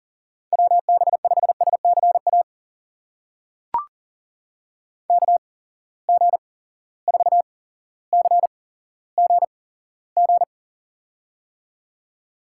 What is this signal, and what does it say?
Morse code, heard as a keyed tone, about 40 wpm, spaced as for 17 wpm, 700 Hz. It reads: WB5SCA KG4CGG
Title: USA Call Signs - 6 Characters - 8x Character Spacing - 40wpm